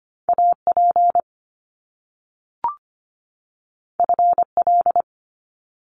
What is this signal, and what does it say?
Morse code, heard as a keyed tone, about 25 wpm, 700 Hz. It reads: AP FL
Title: Top Two Letter Combinations in Words 25wpm